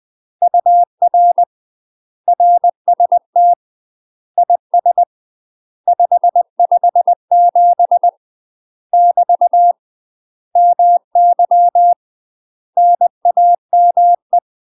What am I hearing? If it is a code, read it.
Morse code, heard as a keyed tone, about 20 wpm, 700 Hz. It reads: UR RST IS 557 <BT> MY NAME
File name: UK MASTER QSOs TRAINING/20WPM